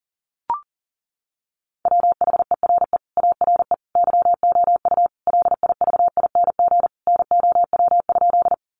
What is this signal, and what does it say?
Morse code, heard as a keyed tone, about 40 wpm, 700 Hz. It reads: WHERE ARE YOU LIVING NOW?
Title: Sentences from Top 1100 Words 40wpm